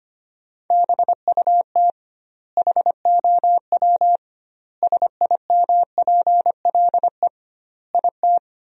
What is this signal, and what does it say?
Morse code, heard as a keyed tone, about 25 wpm, 700 Hz. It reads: BUT HOW SIMPLE IT
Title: 2x Word-Spacing - Sentences from Top 400 Words - 25wpm